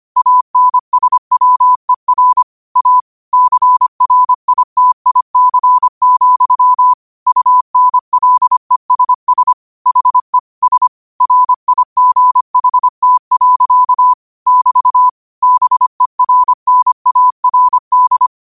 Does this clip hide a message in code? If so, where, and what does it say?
Yes: Morse code from 0.2 to 18.3 s, heard as a keyed tone, about 25 wpm, 1000 Hz. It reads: ANSWER A CRITIC, UNLESS HE S RIGHT. <BT> BERNARD